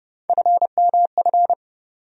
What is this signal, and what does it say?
Morse code, heard as a keyed tone, about 32 wpm, 700 Hz. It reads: FMF
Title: USA Call Sign Postfixes - 3 Letters 30wpm